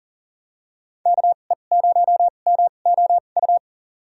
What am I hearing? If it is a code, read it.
Morse code, heard as a keyed tone, about 40 wpm, spaced as for 29 wpm, 700 Hz. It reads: KE0MOU
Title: USA Call Signs - 2x Character Spacing - 40wpm